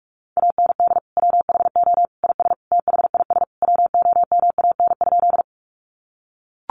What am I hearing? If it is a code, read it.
Morse code, heard as a keyed tone, about 44 wpm, 700 Hz. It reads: AND WHO IS THIS WOMAN?
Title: Sentences from Top 1100 Words 45wpm